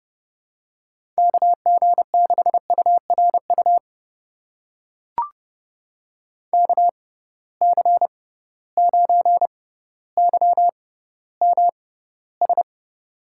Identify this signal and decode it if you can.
Morse code, heard as a keyed tone, about 29 wpm, spaced as for 15 wpm, 700 Hz. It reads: KG6URU KC9YMS
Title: USA Call Signs - 6 Characters - 6x Character Spacing - 30wpm